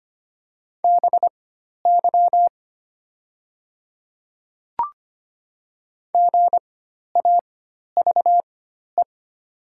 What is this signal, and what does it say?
Morse code, heard as a keyed tone, about 25 wpm, spaced as for 12 wpm, 690 Hz. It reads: BY GAVE
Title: 4x Character Spacing - Top 500 Words - 25wpm